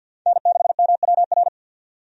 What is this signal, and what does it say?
Morse code, heard as a keyed tone, about 53 wpm, 700 Hz. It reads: N6MWR
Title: USA Call Signs - 5 Characters - 50wpm